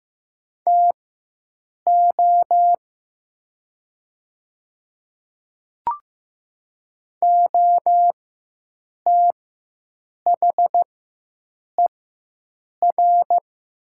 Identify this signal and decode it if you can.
Morse code, heard as a keyed tone, about 15 wpm, spaced as for 7 wpm, 700 Hz. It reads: TO OTHER
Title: 4x Character Spacing - Top 100 Words - 15wpm